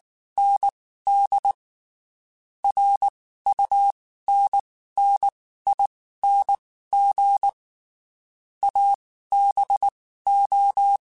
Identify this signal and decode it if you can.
Morse code, heard as a keyed tone, about 19 wpm, spaced as for 14 wpm, 800 Hz. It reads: ND RUNNING ABO
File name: QSOs / QSOs-mp3s / WD0FBT 1@18